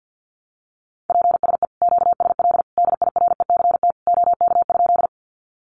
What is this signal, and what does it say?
Morse code, heard as a keyed tone, about 50 wpm, 700 Hz. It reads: PSE QSL DIRECT OK?
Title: Common QSO Phrases 50wpm